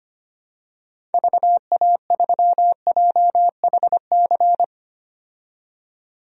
Morse code, heard as a keyed tone, about 25 wpm, 700 Hz. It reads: VA3JHC
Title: DXCC-100 Call Signs - 6 Characters - 25wpm